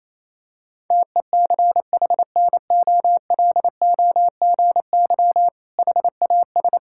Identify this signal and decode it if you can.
Morse code, heard as a keyed tone, about 28 wpm, 700 Hz. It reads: TECHNOLOGY HAS
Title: Sentences from Top 1000 Words - Review Part 6 - 28wpm